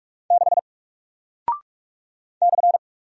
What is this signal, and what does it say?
Morse code, heard as a keyed tone, about 44 wpm, 700 Hz. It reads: 6 /
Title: Single Character (Letter:Number:Punctuation:Prosign) Alt 45wpm